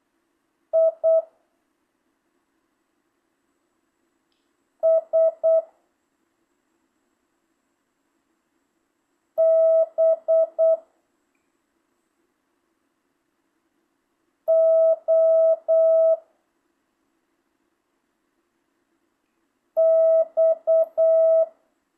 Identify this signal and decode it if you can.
Morse code, heard as a keyed tone, about 8 wpm, 660 Hz. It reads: I S B O X
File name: jh8cbh / 2025cw